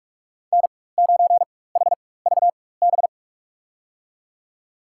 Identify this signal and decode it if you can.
Morse code, heard as a keyed tone, about 42 wpm, spaced as for 20 wpm, 700 Hz. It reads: N9HVB
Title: USA Call Signs - 5 Characters - 4x Character Spacing - 45wpm